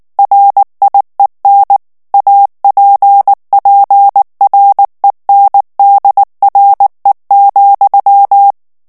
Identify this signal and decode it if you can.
Morse code, heard as a keyed tone, about 19 wpm, 800 Hz. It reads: RIEN APPRENDRE,
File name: CW / FICHIERS-MP3 / FRANCAIS TEXTE CLAIR 2 VITESSE 18 MOTS MINUTES